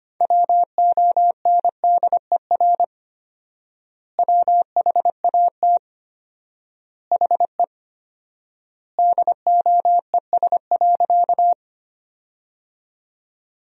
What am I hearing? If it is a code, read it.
Morse code, heard as a keyed tone, about 25 wpm, 700 Hz. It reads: WONDER WHAT HE DOES.
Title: 4x Word-Spacing - Sentences from Top 500 Words - 25wpm